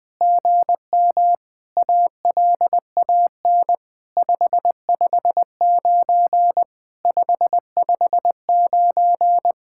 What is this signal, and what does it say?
Morse code, heard as a keyed tone, about 20 wpm, 700 Hz. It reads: GM ALAN 559 559
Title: SOTA QSO Practice 20wpm